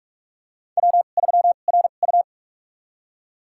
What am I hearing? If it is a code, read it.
Morse code, heard as a keyed tone, about 46 wpm, spaced as for 32 wpm, 700 Hz. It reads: W3RU
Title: USA Call Signs - 4 Characters - 2x Character Spacing - 45wpm